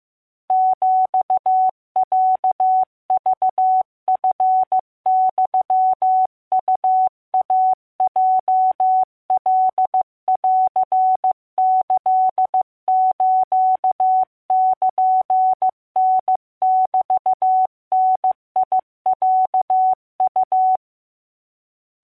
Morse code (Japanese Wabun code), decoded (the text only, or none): ヒロクチユウイヲカンキスルタメダロウ